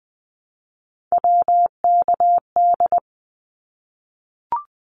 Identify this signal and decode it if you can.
Morse code, heard as a keyed tone, about 19 wpm, 700 Hz. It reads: WKD
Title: Abbreviations Repeated 3x - 20wpm